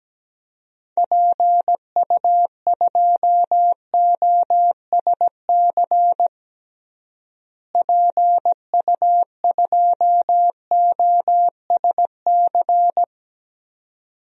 Morse code, heard as a keyed tone, about 17 wpm, 700 Hz. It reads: PU2OSC PU2OSC
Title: DXCC-100 Call Signs - 6 Characters - Repeated 2x - 17wpm